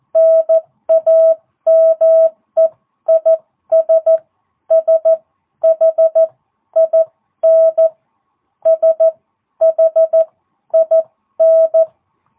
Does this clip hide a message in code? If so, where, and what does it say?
Yes: Morse code from 0.2 to 11.8 s, heard as a keyed tone, about 14 wpm, spaced as for 11 wpm, 660 Hz. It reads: NAME IS SHIN SHIN